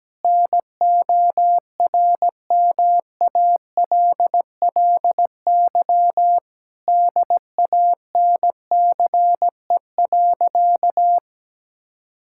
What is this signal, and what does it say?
Morse code, heard as a keyed tone, about 17 wpm, 700 Hz. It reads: NORMALLY DANCE.
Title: Sentences from Top 1000 Words - Review Part 6 - 17wpm